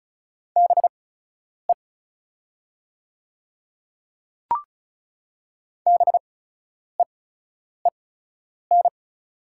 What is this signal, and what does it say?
Morse code, heard as a keyed tone, about 34 wpm, spaced as for 10 wpm, 690 Hz. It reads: BE BEEN
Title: 8x Character Spacing - Top 100 Words - 35wpm